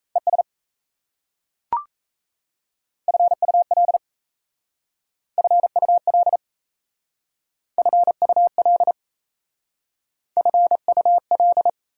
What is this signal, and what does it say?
Morse code, heard as a keyed tone, about 37 wpm, 690 Hz. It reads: ES FUL FUL FUL FUL